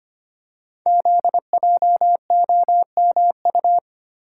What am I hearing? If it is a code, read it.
Morse code, heard as a keyed tone, about 25 wpm, 700 Hz. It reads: ZJOMU
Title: Code Groups - 5 Letters - 25wpm